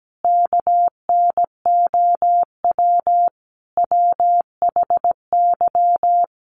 Morse code, heard as a keyed tone, about 17 wpm, 700 Hz. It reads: KNOW WHY
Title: Sets of 4 Words 17wpm